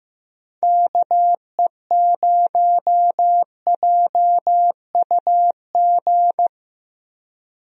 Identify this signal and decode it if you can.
Morse code, heard as a keyed tone, about 15 wpm, 700 Hz. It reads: KE0JUG
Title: USA Call Signs - 6 Characters - 15wpm